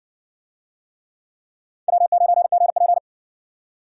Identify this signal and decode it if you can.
Morse code, heard as a keyed tone, about 60 wpm, 700 Hz. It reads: W9GP